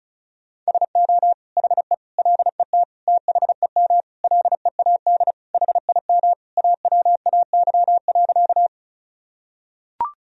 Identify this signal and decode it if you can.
Morse code, heard as a keyed tone, about 34 wpm, 700 Hz. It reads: SO HE LET THEM LEAD HIM AWAY.
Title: Sentences from Top 1000 Words - Review Part 2 - 35wpm